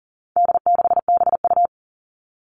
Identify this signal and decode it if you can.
Morse code, heard as a keyed tone, about 40 wpm, 700 Hz. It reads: D6BU
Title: All Cut Numbers - 4-Digit 40wpm